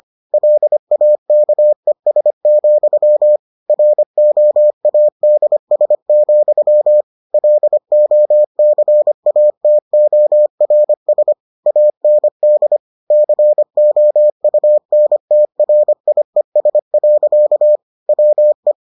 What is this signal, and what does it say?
Morse code, heard as a keyed tone, about 25 wpm, 600 Hz. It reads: LAKES, ROADS, LOCATORS AND COUNTRIES. WE